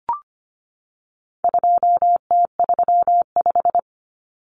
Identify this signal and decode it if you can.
Morse code, heard as a keyed tone, about 25 wpm, 700 Hz. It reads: 2T35
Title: Common Cut Numbers - 4-Digit 25wpm